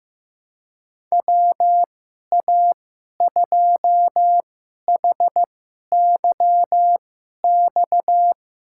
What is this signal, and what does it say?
Morse code, heard as a keyed tone, about 15 wpm, spaced as for 11 wpm, 700 Hz. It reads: WA2HYX
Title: USA Call Signs - 6 Characters - 2x Character Spacing - 15wpm